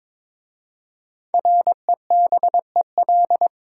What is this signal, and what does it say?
Morse code, heard as a keyed tone, about 22 wpm, 700 Hz. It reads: REBEL
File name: 5 Letter Words - First 3 Letters Given - 22wpm